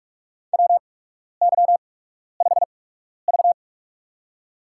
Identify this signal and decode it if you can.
Morse code, heard as a keyed tone, about 44 wpm, spaced as for 12 wpm, 690 Hz. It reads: WY5V
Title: USA Call Signs - 4 Characters - 8x Character Spacing - 45wpm